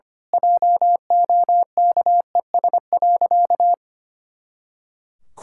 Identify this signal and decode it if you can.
Morse code, heard as a keyed tone, about 25 wpm, 700 Hz. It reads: JOKES.
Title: Pangram Challenge v2 - 25wpm